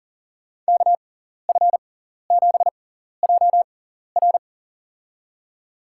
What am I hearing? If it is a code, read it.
Morse code, heard as a keyed tone, about 40 wpm, spaced as for 14 wpm, 700 Hz. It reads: KF7JR
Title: USA Call Signs - 5 Characters - 6x Character Spacing - 40wpm